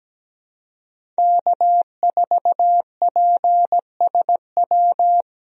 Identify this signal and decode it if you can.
Morse code, heard as a keyed tone, about 17 wpm, 700 Hz. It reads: K4PSW
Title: USA Call Signs - 5 Characters - 17wpm